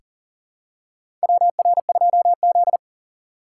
Morse code, heard as a keyed tone, about 40 wpm, 700 Hz. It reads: WR2Z
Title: USA Call Signs - 3x Character Spacing - 40wpm